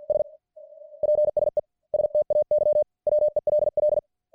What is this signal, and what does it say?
Morse code, heard as a keyed tone, about 50 wpm, 600 Hz. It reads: S PSE STAY WELL